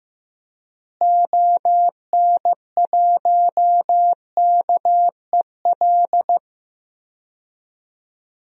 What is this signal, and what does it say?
Morse code, heard as a keyed tone, about 15 wpm, 700 Hz. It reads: ON1KEL
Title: DXCC-100 Call Signs - 6 Characters - Repeated 3x - 15wpm